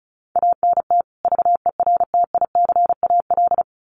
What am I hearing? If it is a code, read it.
Morse code, heard as a keyed tone, about 34 wpm, 700 Hz. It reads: ANT VERTICAL
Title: Common QSO Phrases 35wpm